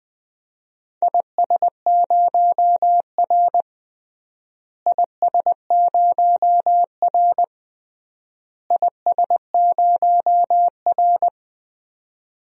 Morse code, heard as a keyed tone, about 20 wpm, 700 Hz. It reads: IS0R IS0R IS0R
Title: DXCC-100 Call Signs - 4 Characters - Repeated 3x - 20wpm